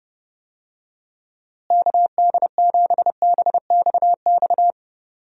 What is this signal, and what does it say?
Morse code, heard as a keyed tone, about 30 wpm, 700 Hz. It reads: KD7BXX
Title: USA Call Signs - 6 Characters - Repeated 3x - 30wpm